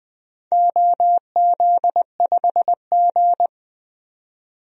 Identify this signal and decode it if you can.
Morse code, heard as a keyed tone, about 20 wpm, 700 Hz. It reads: OZ5G
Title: DXCC-100 Call Signs - 4 Characters - 20wpm